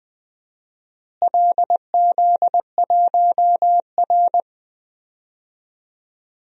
Morse code, heard as a keyed tone, about 20 wpm, 700 Hz. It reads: LZ1R
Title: DXCC-100 Call Signs - 4 Characters - Repeated 3x - 20wpm